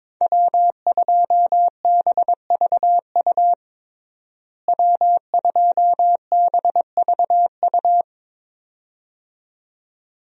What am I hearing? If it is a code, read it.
Morse code, heard as a keyed tone, about 22 wpm, 700 Hz. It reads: W2BVU W2BVU
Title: USA Call Signs - 5 Characters - Repeated 2x - 22wpm